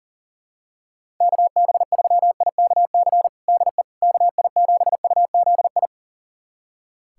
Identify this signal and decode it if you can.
Morse code, heard as a keyed tone, about 40 wpm, 700 Hz. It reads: KB3IKC DE KI7UZI
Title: USA Callsign DE Callsign - 6 Characters - 40wpm